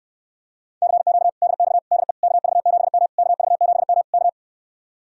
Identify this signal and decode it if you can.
Morse code, heard as a keyed tone, about 67 wpm, 700 Hz. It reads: CQ DX DE KV7M KV7M K